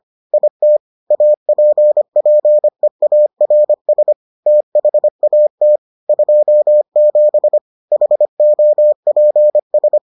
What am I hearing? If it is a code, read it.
Morse code, heard as a keyed tone, about 25 wpm, 600 Hz. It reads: IT APPEARS THAT 27 HOPS